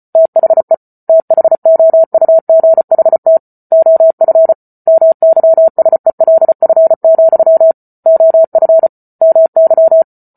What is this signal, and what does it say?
Morse code, heard as a keyed tone, about 34 wpm, 650 Hz. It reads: THE THOUGHT OF MYSELF, OF MY
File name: Book Two / 08-Chapter 8 - 34 WPM